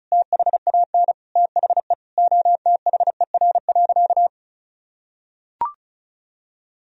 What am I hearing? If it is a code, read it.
Morse code, heard as a keyed tone, about 34 wpm, 700 Hz. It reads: THAN THE OTHER.
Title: Sentences from Top 1000 Words - Review Part 2 - 35wpm